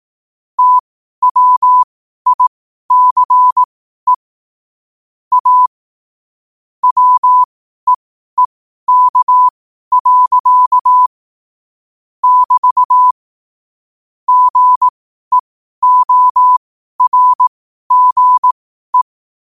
Morse code, heard as a keyed tone, about 18 wpm, spaced as for 12 wpm, 1000 Hz. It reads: TWICE A WEEK. <BT> GEORGE
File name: qotdfiles / 12wpm